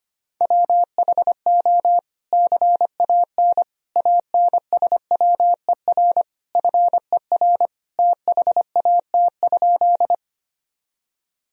Morse code, heard as a keyed tone, about 25 wpm, 700 Hz. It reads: WHO CAN ANSWER FER THAT?
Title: Sentences from Top 300 Words and Abbreviations 25wpm